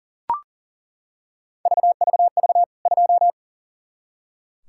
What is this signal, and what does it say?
Morse code, heard as a keyed tone, about 40 wpm, 700 Hz. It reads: VVV 2
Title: K1USN SST QSO Practice 40wpm